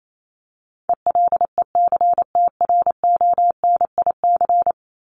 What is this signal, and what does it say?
Morse code, heard as a keyed tone, about 28 wpm, 700 Hz. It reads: ELECTRONIC
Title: Top 1001-1250 Words 28wpm